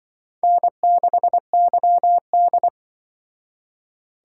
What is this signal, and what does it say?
Morse code, heard as a keyed tone, about 25 wpm, 700 Hz. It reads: N6YD
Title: Speed Racing - USA 1x2 Call Signs 20wpm